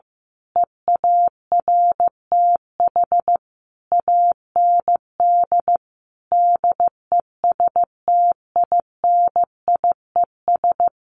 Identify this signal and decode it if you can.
Morse code, heard as a keyed tone, about 15 wpm, 700 Hz. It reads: EARTH AND DESTINIES
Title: A Sound Of Thunder 15wpm